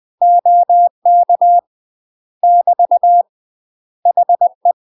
Morse code, heard as a keyed tone, about 20 wpm, 700 Hz. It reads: OK <BT> HE
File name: UK MASTER QSOs TRAINING / 20WPM / UK MASTER QSOs 0025